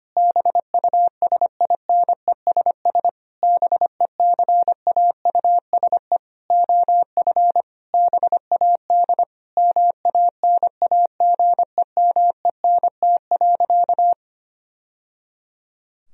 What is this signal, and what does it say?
Morse code, heard as a keyed tone, about 25 wpm, 700 Hz. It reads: BUSINESS BECAUSE OF BAD MANAGEMENT.
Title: Sentences from Top 1000 Words - Review Part 4 - 25wpm